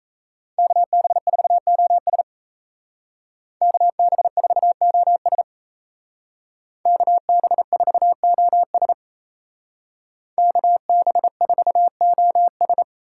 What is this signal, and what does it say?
Morse code, heard as a keyed tone, about 37 wpm, 700 Hz. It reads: KB4OS KB4OS KB4OS KB4OS